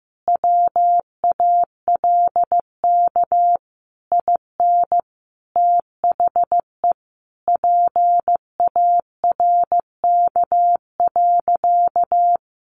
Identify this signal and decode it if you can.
Morse code, heard as a keyed tone, about 15 wpm, 700 Hz. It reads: WALK IN THE PARK.
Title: Sentences from Top 1300 Words 15wpm